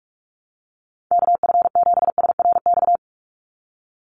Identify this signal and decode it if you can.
Morse code, heard as a keyed tone, about 46 wpm, 690 Hz. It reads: KF7SRX
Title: USA Call Signs 45wpm